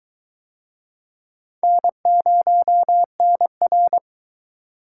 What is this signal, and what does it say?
Morse code, heard as a keyed tone, about 22 wpm, 700 Hz. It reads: N0NR